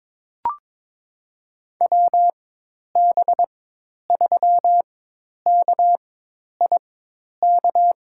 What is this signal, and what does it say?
Morse code, heard as a keyed tone, about 22 wpm, spaced as for 10 wpm, 700 Hz. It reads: WB3KIK
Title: USA Call Signs - 6 Characters - 4x Character Spacing - 22wpm